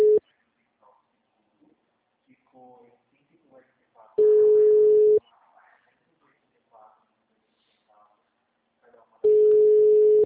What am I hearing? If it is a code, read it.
Ringback tone (Europe, 425 Hz), with speech.
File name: out- -2013- - - .26957